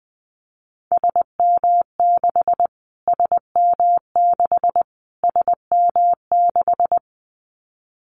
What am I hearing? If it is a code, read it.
Morse code, heard as a keyed tone, about 20 wpm, 700 Hz. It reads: SM6 SM6 SM6
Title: Callsign Prefixes for Top 25 DXCC - Repeated 3x - 20wpm